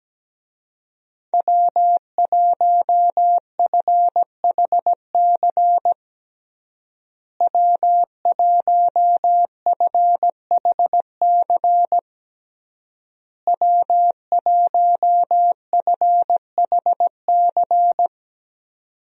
Morse code, heard as a keyed tone, about 17 wpm, 700 Hz. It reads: W1FHC W1FHC W1FHC
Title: USA Call Signs - 5 Characters - Repeated 3x - 17wpm